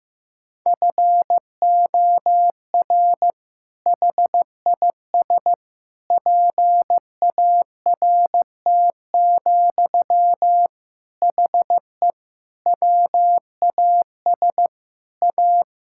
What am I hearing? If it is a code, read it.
Morse code, heard as a keyed tone, about 15 wpm, 700 Hz. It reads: FOR HIS PART, HE WAS A
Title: Sentences from Top 1000 Words - Review Part 5 - 15wpm